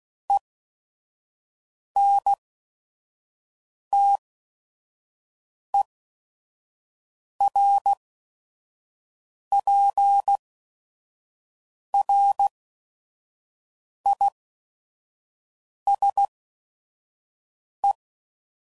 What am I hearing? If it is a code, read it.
Morse code, heard as a keyed tone, about 16 wpm, spaced as for 5 wpm, 800 Hz. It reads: ENTERPRISE
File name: CW Code Groups @5WPM / fromSM's / Cities